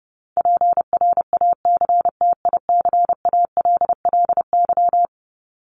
Morse code, heard as a keyed tone, about 28 wpm, 700 Hz. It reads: PRACTICALLY
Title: LY Suffix - Top 100 Words with Base Words 30wpm